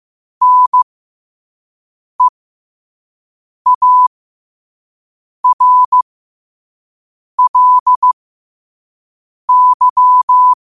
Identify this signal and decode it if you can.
Morse code, heard as a keyed tone, about 15 wpm, spaced as for 5 wpm, 1000 Hz. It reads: NEARLY